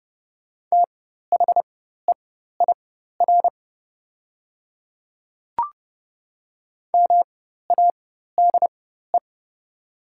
Morse code, heard as a keyed tone, about 30 wpm, spaced as for 14 wpm, 700 Hz. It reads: THEIR MADE
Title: 4x Character Spacing - Top 200 Words - 30wpm